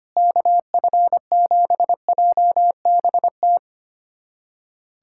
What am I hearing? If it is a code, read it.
Morse code, heard as a keyed tone, about 25 wpm, 700 Hz. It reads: KF7JBT
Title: USA Callsign DE Callsign 25wpm